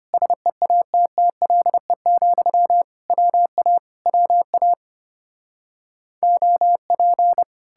Morse code, heard as a keyed tone, about 33 wpm, 700 Hz. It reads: SEATTLE, WA WA OP